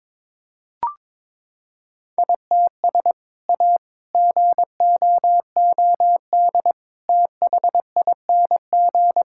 Morse code, heard as a keyed tone, about 21 wpm, 700 Hz. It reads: ITS A GOOD THING
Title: Sentences from Top 1000 Words - Review Part 5 - 22wpm